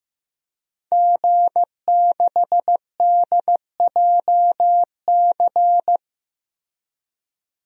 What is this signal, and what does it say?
Morse code, heard as a keyed tone, about 15 wpm, 700 Hz. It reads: G6DJC
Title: DXCC-100 Call Signs - 5 Characters - 15wpm